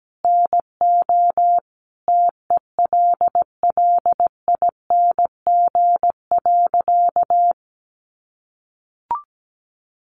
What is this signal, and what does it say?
Morse code, heard as a keyed tone, about 17 wpm, 700 Hz. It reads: NO TELLING.
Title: A Sound Of Thunder 17wpm